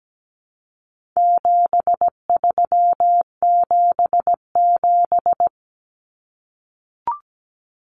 Morse code, heard as a keyed tone, about 17 wpm, 700 Hz. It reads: 7377
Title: Common Cut Numbers - 4-Digit 17wpm